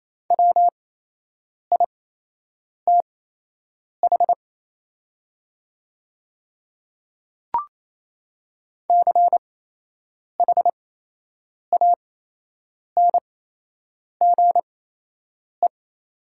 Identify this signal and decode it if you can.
Morse code, heard as a keyed tone, about 27 wpm, spaced as for 8 wpm, 700 Hz. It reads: WITH CHANGE
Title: 8x Character Spacing - Top 200 Words - 28wpm